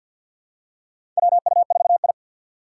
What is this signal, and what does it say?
Morse code, heard as a keyed tone, about 51 wpm, 690 Hz. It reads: WU4I
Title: USA Call Signs - 4 Characters - 6x Character Spacing - 50wpm